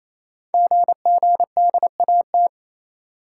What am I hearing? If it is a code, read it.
Morse code, heard as a keyed tone, about 27 wpm, 700 Hz. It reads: GGDAT
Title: Code Groups - 5 Letters - 28wpm